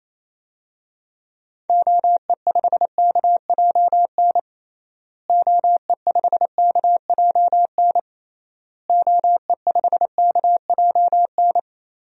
Morse code, heard as a keyed tone, about 28 wpm, 700 Hz. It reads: OE5KJN OE5KJN OE5KJN
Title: DXCC-100 Call Signs - 6 Characters - Repeated 3x - 28wpm